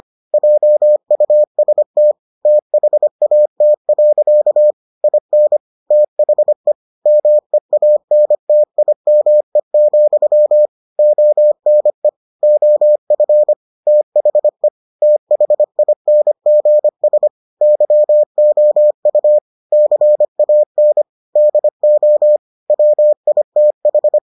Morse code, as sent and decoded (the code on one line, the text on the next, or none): .--- ..- ... - / - .... .- - .-.-.- / .. -. / - .... / -- . .- -. - .. -- . --..-- / --- -. / --- ..-. / - .... / - .... .. -. --. ... / -.-- --- ..- / -.-. .- -. / -.. --- / .-- .. - ....
JUST THAT. IN THE MEANTIME, ONE OF THE THINGS YOU CAN DO WITH